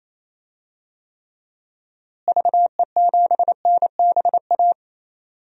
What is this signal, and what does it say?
Morse code, heard as a keyed tone, about 28 wpm, 700 Hz. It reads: VE7NBA
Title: DXCC-100 Call Signs - 6 Characters - Repeated 2x - 28wpm